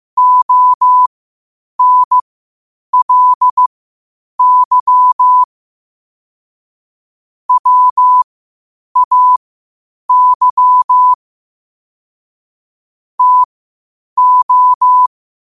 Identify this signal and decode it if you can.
Morse code, heard as a keyed tone, about 15 wpm, spaced as for 8 wpm, 1000 Hz. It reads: ONLY WAY TO
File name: qotdfiles / 8wpm